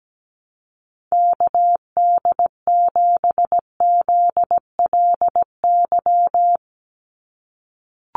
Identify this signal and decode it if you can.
Morse code, heard as a keyed tone, about 17 wpm, 700 Hz. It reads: KD7ZLY
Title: USA Call Signs 17wpm